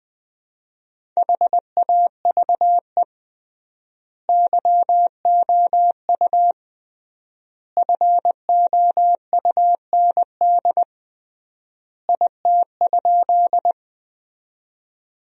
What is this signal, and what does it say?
Morse code, heard as a keyed tone, about 20 wpm, 700 Hz. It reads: HAVE YOU FOUND IT?
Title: 3x Word-Spacing - Sentences from Top 300 Words - 20wpm